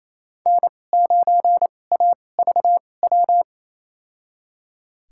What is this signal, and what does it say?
Morse code, heard as a keyed tone, about 28 wpm, spaced as for 20 wpm, 700 Hz. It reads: N9AVW
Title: USA Call Signs - 5 Characters - 2x Character Spacing - 28wpm